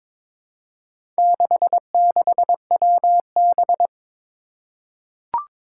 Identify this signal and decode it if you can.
Morse code, heard as a keyed tone, about 22 wpm, 700 Hz. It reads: 66WB
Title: Four Challenging Letter-Number Combinations 22wpm